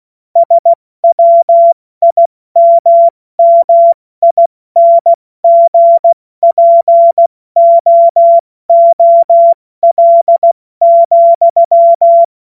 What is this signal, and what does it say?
Morse code, heard as a keyed tone, about 16 wpm, 680 Hz. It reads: SWIMMINGPOOL,
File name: Morsekurs / Witze / Witz 29. November 2022 70 F80